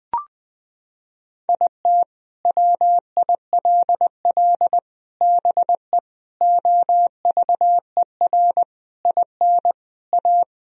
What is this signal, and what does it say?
Morse code, heard as a keyed tone, about 20 wpm, 700 Hz. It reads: IT WILL BE OVER IN A
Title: Sentences from Top 1000 Words - Review Part 2 - 20wpm